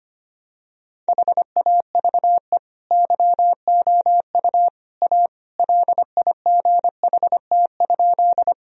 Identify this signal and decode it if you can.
Morse code, heard as a keyed tone, about 25 wpm, 700 Hz. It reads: HAVE YOU A LIGHT?
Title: Sentences from Top 1000 Words - Review Part 1 - 25wpm